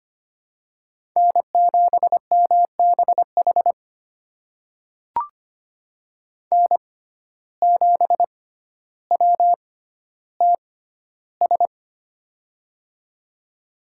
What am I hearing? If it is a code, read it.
Morse code, heard as a keyed tone, about 25 wpm, spaced as for 13 wpm, 700 Hz. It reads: N7MBH N7WTS
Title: USA Call Signs - 5 Characters - 6x Character Spacing - 25wpm